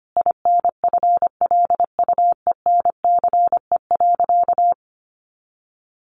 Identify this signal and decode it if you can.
Morse code, heard as a keyed tone, about 25 wpm, 700 Hz. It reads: INFLUENCE.
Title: Sentences from Top 1100 Words 25wpm